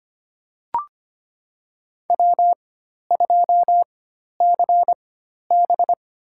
Morse code, heard as a keyed tone, about 25 wpm, spaced as for 12 wpm, 690 Hz. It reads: W2CB
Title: USA Call Signs - 4 Characters - 4x Character Spacing - 25wpm